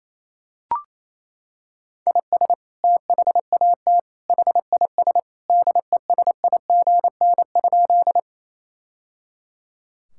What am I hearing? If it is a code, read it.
Morse code, heard as a keyed tone, about 28 wpm, 700 Hz. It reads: IS THAT HIS DESIGN?
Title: Sentences from Top 1000 Words - Review Part 5 - 28wpm